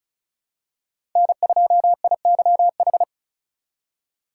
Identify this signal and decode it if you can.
Morse code, heard as a keyed tone, about 36 wpm, 700 Hz. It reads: N2IYH
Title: USA Call Signs - 5 Characters - 4x Character Spacing - 35wpm